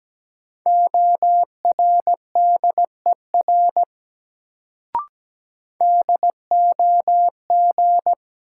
Morse code, heard as a keyed tone, about 17 wpm, 700 Hz. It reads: ORDER DOG
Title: Sentences from Top 400 Words - New Word Drill - 17wpm